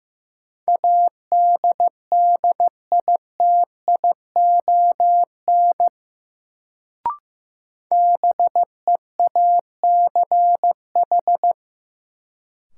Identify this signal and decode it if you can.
Morse code, heard as a keyed tone, about 15 wpm, 700 Hz. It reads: ADDITION BEACH
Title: Sentences from Top 1500 Words - New Word Drill Alt - 15wpm